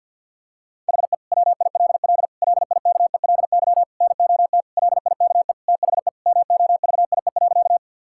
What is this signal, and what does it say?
Morse code, heard as a keyed tone, about 50 wpm, 700 Hz. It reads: HE WILL LIKELY NOT LIKE THE MOVIE.
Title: Sentences from Top 1000 Words - Review Part 5 - 50wpm